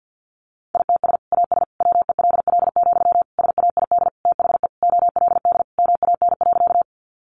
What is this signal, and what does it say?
Morse code, heard as a keyed tone, about 48 wpm, 700 Hz. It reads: ITS AS WELL, SAID THE OLD MAN.
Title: Sentences from Top 600 Words 50wpm